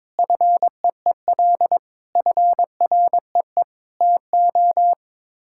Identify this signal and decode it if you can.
Morse code, heard as a keyed tone, about 22 wpm, 700 Hz. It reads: FEEL FREE TO
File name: practice / 22wpm / Sets of 8 Words - 22wpm